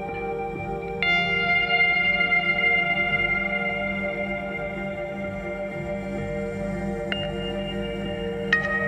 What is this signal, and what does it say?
Dial-up modem handshake.